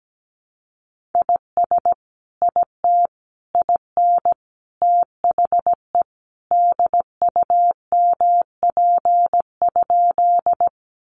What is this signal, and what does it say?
Morse code, heard as a keyed tone, about 17 wpm, 700 Hz. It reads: IS IT IN THE DUMP?
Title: Sentences from Top 1300 Words 17wpm